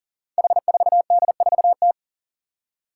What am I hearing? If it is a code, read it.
Morse code, heard as a keyed tone, about 40 wpm, 700 Hz. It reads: H4D4T
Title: Five Challenging Letter-Number Combinations 40wpm